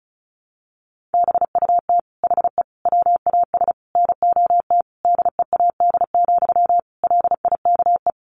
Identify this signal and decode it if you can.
Morse code, heard as a keyed tone, about 34 wpm, 700 Hz. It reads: BUT HE WAS NOT DEAD, LIKE
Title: Sentences from Top 1500 Words 35wpm